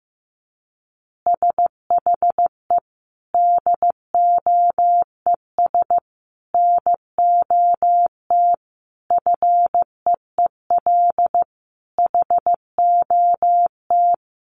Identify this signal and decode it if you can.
Morse code, heard as a keyed tone, about 15 wpm, 700 Hz. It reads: SHE DOES NOT FEEL HOT
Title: Easy Phrases 15wpm